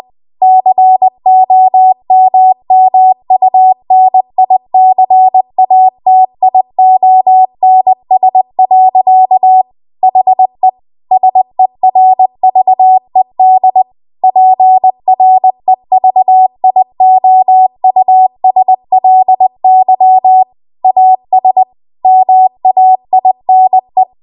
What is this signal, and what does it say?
Morse code, heard as a keyed tone, about 20 wpm, 750 Hz. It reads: COMMUNICATIONS. HE SERVED PREVIOUSLY AS MAINE
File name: Archive / 20 WPM / 20WPM